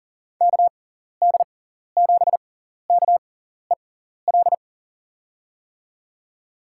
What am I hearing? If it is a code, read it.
Morse code, heard as a keyed tone, about 40 wpm, spaced as for 14 wpm, 700 Hz. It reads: KD7KEL
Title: USA Call Signs - 6 Characters - 6x Character Spacing - 40wpm